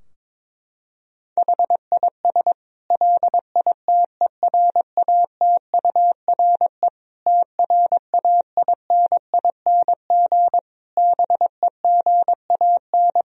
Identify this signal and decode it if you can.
Morse code, heard as a keyed tone, about 21 wpm, 700 Hz. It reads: HIS LITERATURE TRAINING BEGAN
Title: Sentences from Top 1000 Words - Review Part 4 - 22wpm